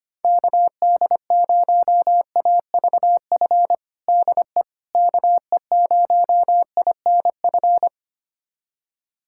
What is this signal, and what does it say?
Morse code, heard as a keyed tone, about 25 wpm, 700 Hz. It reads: KD0AVF DE KE0INF
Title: USA Callsign DE Callsign 25wpm